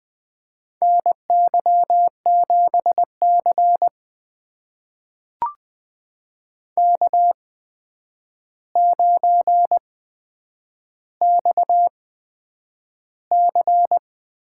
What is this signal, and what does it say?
Morse code, heard as a keyed tone, about 20 wpm, spaced as for 9 wpm, 700 Hz. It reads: NY7C K9XC
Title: USA Call Signs - 4 Characters - 8x Character Spacing - 20wpm